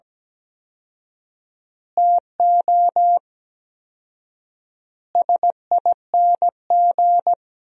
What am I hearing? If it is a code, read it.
Morse code, heard as a keyed tone, about 17 wpm, 700 Hz. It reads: TO SING
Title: 4x Word-Spacing - Sentences from Top 400 Words - 17wpm